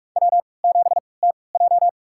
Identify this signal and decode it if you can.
Morse code, heard as a keyed tone, about 46 wpm, spaced as for 25 wpm, 700 Hz. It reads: W7TJ
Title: USA Call Signs - 4 Characters - 3x Character Spacing - 45wpm